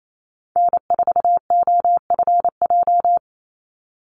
Morse code, heard as a keyed tone, about 28 wpm, 700 Hz. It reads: N4OFJ
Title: USA Call Signs 28wpm